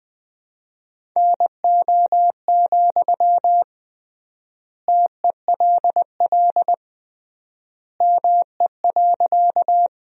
Morse code, heard as a keyed tone, about 20 wpm, 700 Hz. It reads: NO, TELL ME.
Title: 3x Word-Spacing - Sentences from Top 200 Words - 20wpm